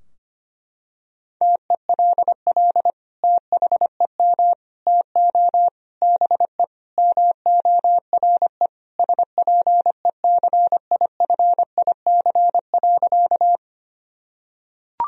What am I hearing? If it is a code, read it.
Morse code, heard as a keyed tone, about 25 wpm, 700 Hz. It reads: TELL THEM TO BE MORE SPECIFIC.
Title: Sentences from Top 1000 Words - Review Part 3 - 25wpm